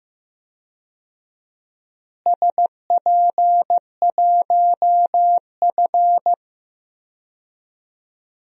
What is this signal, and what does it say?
Morse code, heard as a keyed tone, about 15 wpm, 700 Hz. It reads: SP1F
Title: DXCC-100 Call Signs - 4 Characters - Repeated 2x - 15wpm